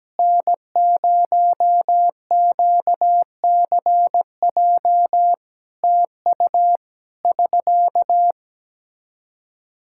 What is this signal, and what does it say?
Morse code, heard as a keyed tone, about 17 wpm, 700 Hz. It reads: N0QCJ TU <SK>
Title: Practice QSO - Exchange 4 - Ending - 17wpm